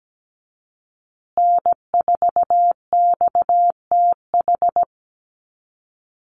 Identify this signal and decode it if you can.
Morse code, heard as a keyed tone, about 17 wpm, 700 Hz. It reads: N4XTH
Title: USA Call Signs Repeated 3x 17wpm